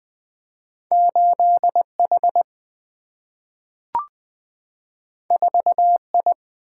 Morse code, heard as a keyed tone, about 20 wpm, 700 Hz. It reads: 8 H 4 I
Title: Single-Letter Single-Number 20wpm